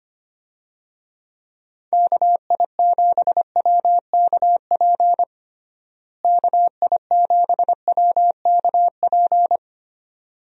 Morse code, heard as a keyed tone, about 25 wpm, 700 Hz. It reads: KI7WKP KI7WKP
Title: USA Call Signs - 6 Characters - Repeated 2x - 25wpm